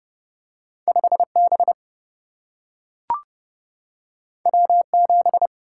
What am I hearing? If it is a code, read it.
Morse code, heard as a keyed tone, about 30 wpm, 700 Hz. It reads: 5 B W 7
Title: Single-Letter Single-Number 30wpm